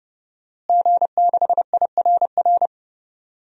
Morse code, heard as a keyed tone, about 30 wpm, 700 Hz. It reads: G6IRR
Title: DXCC-100 Call Signs - 5 Characters - Repeated 3x - 30wpm